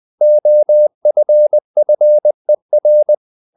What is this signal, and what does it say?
Morse code, heard as a keyed tone, about 20 wpm, 600 Hz. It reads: OFFER
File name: the old man and the sea / the old man and the sea part 1 20wpm